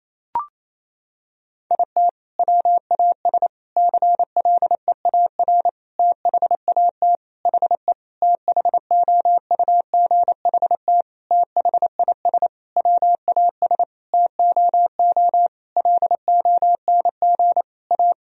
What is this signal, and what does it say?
Morse code, heard as a keyed tone, about 27 wpm, 700 Hz. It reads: IT WAS CLEAR THAT HE THOUGHT THIS WAS TOO LONG A
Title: Sentences from Top 1000 Words - Review Part 4 - 28wpm